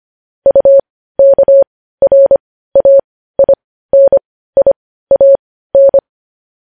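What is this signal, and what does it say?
Morse code, heard as a keyed tone, about 25 wpm, spaced as for 15 wpm, 550 Hz. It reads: UKRAINIAN
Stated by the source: Morse code news headlines practice at 15.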